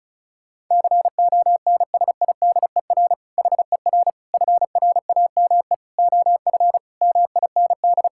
Morse code, heard as a keyed tone, about 36 wpm, 690 Hz. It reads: CONSIDER HER FRAME OF MIND
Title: Sentences from Top 1000 Words - Review Part 6 - 35wpm